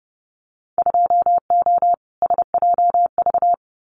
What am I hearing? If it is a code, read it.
Morse code, heard as a keyed tone, about 30 wpm, 700 Hz. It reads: 2O SJV
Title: Winter Field Day Exchanges Encore 30wpm